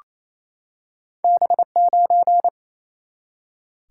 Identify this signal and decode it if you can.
Morse code, heard as a keyed tone, about 28 wpm, 700 Hz. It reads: B9
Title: Single-Letter Single-Number 28wpm